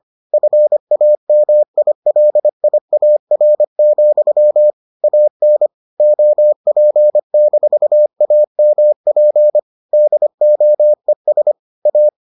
Morse code, heard as a keyed tone, about 25 wpm, 600 Hz. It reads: FAMILIAR, AN OP-AMP DOES A